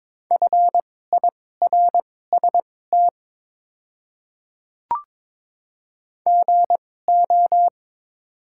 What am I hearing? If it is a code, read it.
Morse code, heard as a keyed tone, about 22 wpm, spaced as for 16 wpm, 700 Hz. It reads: FIRST GO
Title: 2x Character Spacing - Top 100 Words - 22wpm